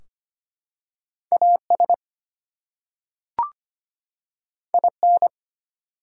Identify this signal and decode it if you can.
Morse code, heard as a keyed tone, about 25 wpm, 700 Hz. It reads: AS IN
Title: Two Letter Words - 25wpm